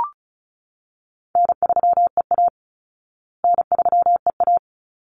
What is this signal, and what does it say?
Morse code, heard as a keyed tone, about 34 wpm, 700 Hz. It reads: N3EA N3EA
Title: Easy Call Signs Repeated 2x - 35wpm